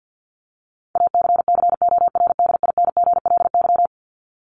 Morse code, heard as a keyed tone, about 50 wpm, 700 Hz. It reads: ACCORDINGLY
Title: LY Suffix - Top 100 Words with Base Words 50wpm